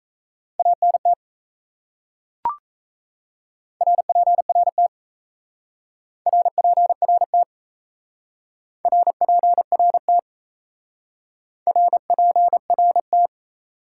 Morse code, heard as a keyed tone, about 37 wpm, 700 Hz. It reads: ANT RPRT RPRT RPRT RPRT